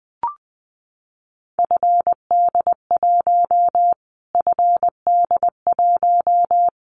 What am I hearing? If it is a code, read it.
Morse code, heard as a keyed tone, about 20 wpm, 700 Hz. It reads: FD1 FD1
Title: Callsign Prefixes for Top 25 DXCC - Repeated 3x - 20wpm